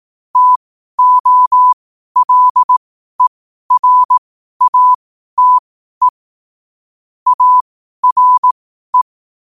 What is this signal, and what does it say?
Morse code, heard as a keyed tone, about 18 wpm, spaced as for 12 wpm, 1000 Hz. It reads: TOLERATE ARE
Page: qotd-at-12-wpm